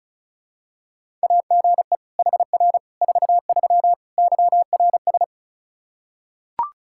Morse code, heard as a keyed tone, about 34 wpm, 700 Hz. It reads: AGE HR 43 YRS
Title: Practice QSO - Exchange 3 - More Info - 35wpm